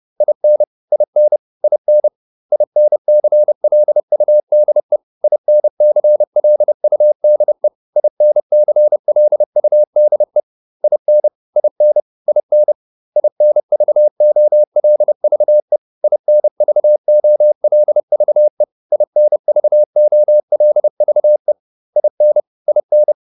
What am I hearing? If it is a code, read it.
Morse code, heard as a keyed tone, about 30 wpm, 600 Hz. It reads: IN IN IN INCLUDE INCLUDE INCLUDE IN IN IN INVOLVE INVOLVE INVOLVE IN IN